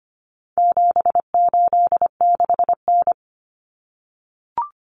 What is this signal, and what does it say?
Morse code, heard as a keyed tone, about 25 wpm, 700 Hz. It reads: 786N
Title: Common Cut Numbers - 4-Digit 25wpm